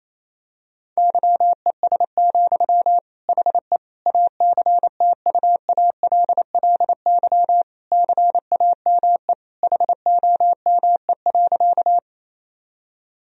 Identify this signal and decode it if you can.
Morse code, heard as a keyed tone, about 27 wpm, 700 Hz. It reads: YES, HE ACTUALLY CAME HOME.
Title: Sentences from Top 1000 Words - Review Part 4 - 28wpm